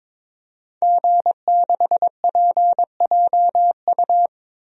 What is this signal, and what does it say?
Morse code, heard as a keyed tone, about 22 wpm, 700 Hz. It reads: G6PJU
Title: DXCC-100 Call Signs - 5 Characters - 22wpm